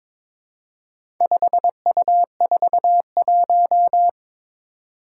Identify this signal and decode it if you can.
Morse code, heard as a keyed tone, about 22 wpm, 700 Hz. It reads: 5U41
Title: Four Challenging Letter-Number Combinations 22wpm